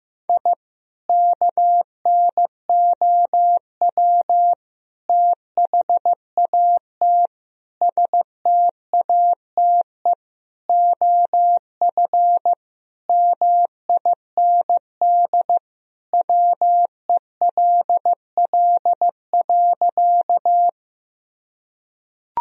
Morse code, heard as a keyed tone, about 15 wpm, 700 Hz. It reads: I KNOW THAT STATE OF MIND WELL.
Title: Sentences from Top 1000 Words - Review Part 2 - 15wpm